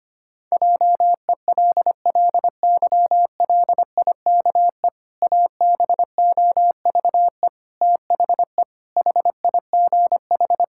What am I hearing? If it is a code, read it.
Morse code, heard as a keyed tone, about 25 wpm, 700 Hz. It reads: JELLYLIKE ABOVE THE HIGH
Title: Pangram Challenge v2 - 25wpm